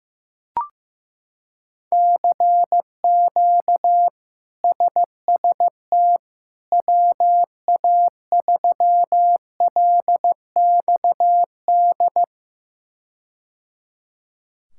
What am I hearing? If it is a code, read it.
Morse code, heard as a keyed tone, about 15 wpm, 700 Hz. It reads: CQ SST WA3LXD
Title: K1USN SST QSO Practice - 15wpm